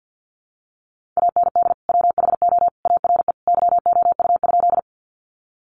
Morse code, heard as a keyed tone, about 50 wpm, 700 Hz. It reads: AND WHO ARE YOU?
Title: Sentences from Top 100 Words 50wpm